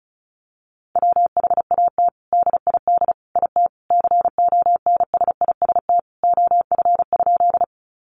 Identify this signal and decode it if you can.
Morse code, heard as a keyed tone, about 34 wpm, 700 Hz. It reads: WHAT DID IT CONSIST OF?
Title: Sentences from Top 1400 Words 35wpm